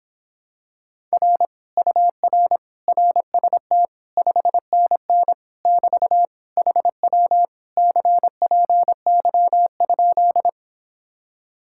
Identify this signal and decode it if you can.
Morse code, heard as a keyed tone, about 26 wpm, 700 Hz. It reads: R UR RST 5NN <BT> HW CPY?